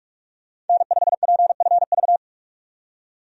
Morse code, heard as a keyed tone, about 42 wpm, 700 Hz. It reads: N5PFV
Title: USA Call Signs - 5 Characters - 45wpm